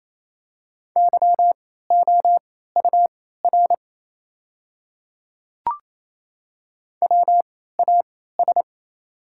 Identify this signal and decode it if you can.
Morse code, heard as a keyed tone, about 28 wpm, spaced as for 16 wpm, 700 Hz. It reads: YOUR WAS
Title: 3x Character Spacing - Top 300 Words - 28wpm